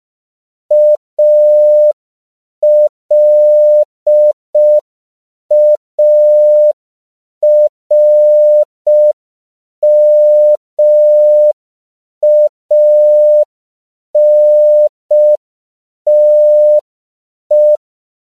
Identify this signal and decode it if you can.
Morse code, heard as a keyed tone, about 5 wpm, 600 Hz. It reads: ALARMANTE
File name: CW / 5PPM